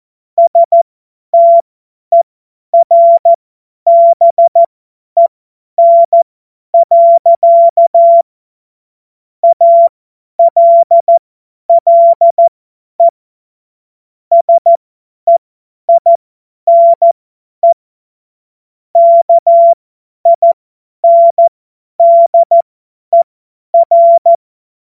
Morse code, heard as a keyed tone, about 14 wpm, spaced as for 10 wpm, 680 Hz. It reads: STERBEN. ALLE SEINE KINDER
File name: Morsekurs / Witze / Witz 16. Februar 2021, 50 F70